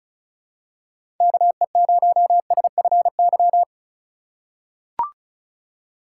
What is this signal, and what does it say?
Morse code, heard as a keyed tone, about 36 wpm, 700 Hz. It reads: KE0SFY
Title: USA Call Signs - 6 Characters - 3x Character Spacing - 35wpm